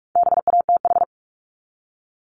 Morse code, heard as a keyed tone, about 44 wpm, 700 Hz. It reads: BATH
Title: Top 1501-1750 Words 45wpm